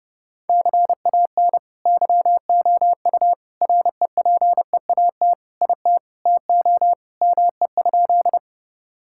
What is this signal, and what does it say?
Morse code, heard as a keyed tone, about 30 wpm, 700 Hz. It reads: CAN YOU REPEAT IT TO ME?
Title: Sentences from Top 1600 Words 30wpm